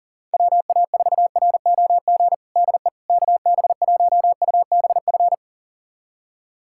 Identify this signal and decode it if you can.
Morse code, heard as a keyed tone, about 40 wpm, 700 Hz. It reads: WA4ROG DE KB1UBF
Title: USA Callsign DE Callsign - 6 Characters - 40wpm